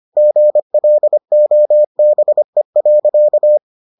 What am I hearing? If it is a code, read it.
Morse code, heard as a keyed tone, about 25 wpm, 600 Hz. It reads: GLOBE.